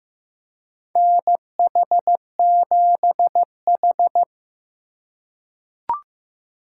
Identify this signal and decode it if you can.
Morse code, heard as a keyed tone, about 15 wpm, 700 Hz. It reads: NH7H
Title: USA Call Signs - 4 Characters - 2x Character Spacing - 15wpm